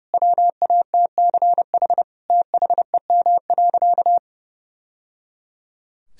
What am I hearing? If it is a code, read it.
Morse code, heard as a keyed tone, about 30 wpm, 700 Hz. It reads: WATCH THEM.
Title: Sentences from Top 1000 Words - Review Part 1 - 30wpm